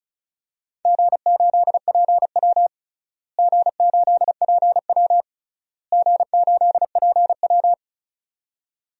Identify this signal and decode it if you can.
Morse code, heard as a keyed tone, about 34 wpm, 700 Hz. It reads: G8PW G8PW G8PW
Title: DXCC-100 Call Signs - 4 Characters - Repeated 3x - 35wpm